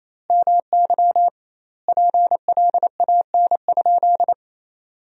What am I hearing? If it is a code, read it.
Morse code, heard as a keyed tone, about 27 wpm, 700 Hz. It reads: MY PLAN?
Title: 2x Word-Spacing - Sentences from Top 500 Words - 28wpm